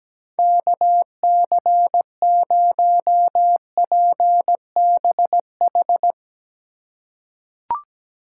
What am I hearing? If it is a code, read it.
Morse code, heard as a keyed tone, about 17 wpm, 700 Hz. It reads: KC0PBH
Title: USA Call Signs - 2x Character Spacing - 17wpm